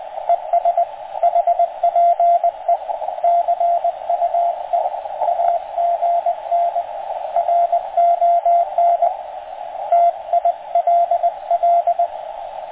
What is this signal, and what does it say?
Morse code, heard as a keyed tone, about 20 wpm, 690 Hz, with radio noise. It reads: ES HPE CUE AGN RON TILL